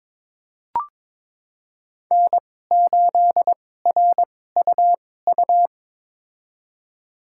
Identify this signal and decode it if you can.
Morse code, heard as a keyed tone, about 22 wpm, spaced as for 16 wpm, 700 Hz. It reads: N8RUU
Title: USA Call Signs - 5 Characters - 2x Character Spacing - 22wpm